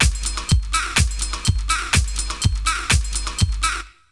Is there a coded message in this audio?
No.